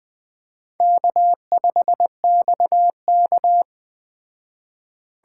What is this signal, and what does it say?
Morse code, heard as a keyed tone, about 20 wpm, 700 Hz. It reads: K5XK
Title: Speed Racing - USA 1x2 Call Signs 17wpm